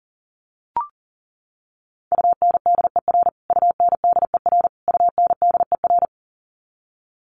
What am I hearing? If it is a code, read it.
Morse code, heard as a keyed tone, about 40 wpm, 700 Hz. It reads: UNDER UNDER UNDER
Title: Top 200 Words Repeated 40wpm